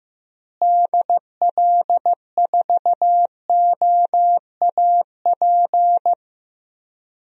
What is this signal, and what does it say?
Morse code, heard as a keyed tone, about 15 wpm, 700 Hz. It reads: DL4OAP
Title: USA Call Signs - 6 Characters - Repeated 3x - 15wpm